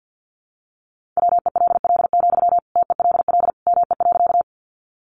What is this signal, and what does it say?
Morse code, heard as a keyed tone, about 50 wpm, 700 Hz. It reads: WELL, TELL ME.
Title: Sentences from Top 200 Words 50wpm